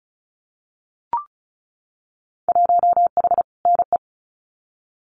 Morse code, heard as a keyed tone, about 34 wpm, 700 Hz. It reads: 1H NE
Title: Winter Field Day Exchanges Encore 35wpm